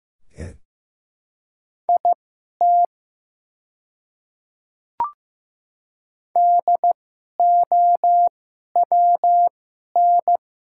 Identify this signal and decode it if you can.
Morse code, heard as a keyed tone, about 15 wpm, spaced as for 11 wpm, 700 Hz. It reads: IT DOWN
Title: 2x Character Spacing - Top 200 Words - 15wpm